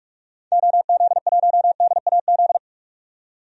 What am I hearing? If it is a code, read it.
Morse code, heard as a keyed tone, about 42 wpm, 700 Hz. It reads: OZ1DAZ
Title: DXCC-100 Call Signs - 6 Characters - 45wpm